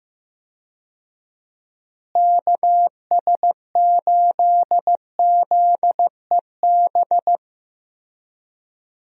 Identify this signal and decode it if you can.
Morse code, heard as a keyed tone, about 15 wpm, 700 Hz. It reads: KS8ZEB
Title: USA Call Signs - 6 Characters - Repeated 3x - 15wpm